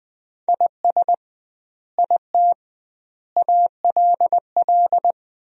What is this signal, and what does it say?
Morse code, heard as a keyed tone, about 21 wpm, 700 Hz. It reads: IS IT ALL
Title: 2x Word-Spacing - Sentences from Top 200 Words - 20wpm